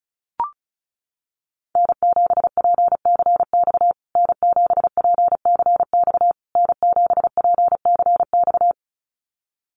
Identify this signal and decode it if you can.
Morse code, heard as a keyed tone, about 34 wpm, 700 Hz. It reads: N7PCX N7PCX N7PCX
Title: USA Call Signs Repeated 3x 35wpm